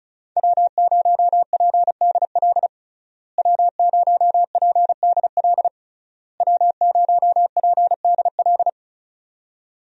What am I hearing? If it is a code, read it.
Morse code, heard as a keyed tone, about 34 wpm, 700 Hz. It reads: W0PDL W0PDL W0PDL
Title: USA Call Signs - 5 Characters - Repeated 3x - 35wpm